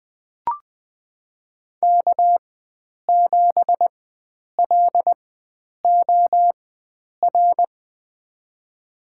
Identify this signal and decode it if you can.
Morse code, heard as a keyed tone, about 20 wpm, spaced as for 9 wpm, 700 Hz. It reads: K7LOR
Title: USA Call Signs - 4x Character Spacing - 20wpm